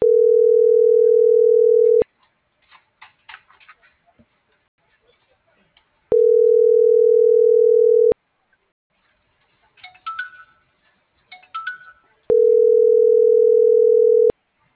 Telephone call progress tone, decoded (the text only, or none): ringback tone